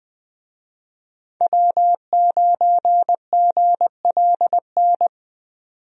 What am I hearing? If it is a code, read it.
Morse code, heard as a keyed tone, about 20 wpm, 700 Hz. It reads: W9GLN
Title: USA Call Signs - 5 Characters - 20wpm